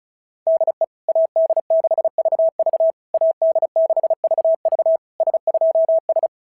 Morse code, heard as a keyed tone, about 33 wpm, 660 Hz. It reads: DE AD6VV AD6VV S2S
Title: SOTA QSO Practice 35wpm